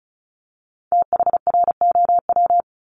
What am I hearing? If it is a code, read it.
Morse code, heard as a keyed tone, about 36 wpm, 700 Hz. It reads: THROW
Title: Multi-letter Phonemes 35wpm